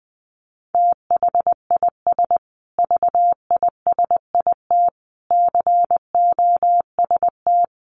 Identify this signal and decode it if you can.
Morse code, heard as a keyed tone, about 20 wpm, 700 Hz. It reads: THIS VISIT COST
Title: Sentences from Top 1400 Words 20wpm